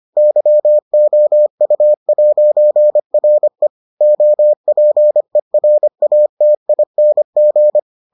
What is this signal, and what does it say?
Morse code, heard as a keyed tone, about 25 wpm, 600 Hz. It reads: YOU'RE OPERATING